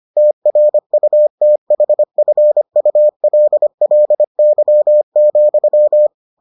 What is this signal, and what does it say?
Morse code, heard as a keyed tone, about 25 wpm, 600 Hz. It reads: TRUTHFULLY,